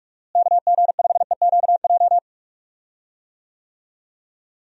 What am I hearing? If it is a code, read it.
Morse code, heard as a keyed tone, about 46 wpm, 700 Hz. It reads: KG5EQJ
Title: USA Call Signs - 6 Characters - Repeated 3x - 45wpm